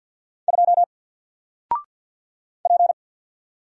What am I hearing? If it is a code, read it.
Morse code, heard as a keyed tone, about 49 wpm, 700 Hz. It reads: 2 P
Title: Single Character (Letter:Number:Punctuation:Prosign) Alt 50wpm